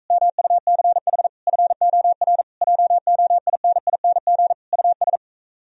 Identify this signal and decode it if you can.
Morse code, heard as a keyed tone, about 45 wpm, 700 Hz. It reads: MUCH FOR JOINING US